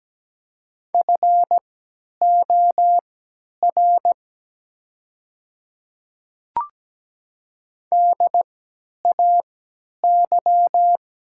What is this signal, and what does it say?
Morse code, heard as a keyed tone, about 17 wpm, spaced as for 10 wpm, 700 Hz. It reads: FOR DAY
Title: 3x Character Spacing - Top 200 Words - 17wpm